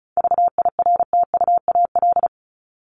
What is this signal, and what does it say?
Morse code, heard as a keyed tone, about 34 wpm, 700 Hz. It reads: VIRTUAL
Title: LY Suffix - Top 100 Words with Base Words 35wpm